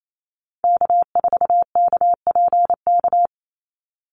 Morse code, heard as a keyed tone, about 28 wpm, 700 Hz. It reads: K4KPK
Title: NA SOTA Activator Callsigns 28wpm